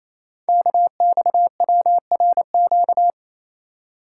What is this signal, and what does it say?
Morse code, heard as a keyed tone, about 27 wpm, 700 Hz. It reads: KXWRQ
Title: Code Groups - 5 Letters - 28wpm